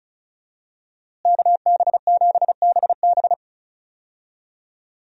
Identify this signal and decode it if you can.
Morse code, heard as a keyed tone, about 34 wpm, 700 Hz. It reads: KB7BB
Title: USA Call Signs - 5 Characters - Repeated 2x - 35wpm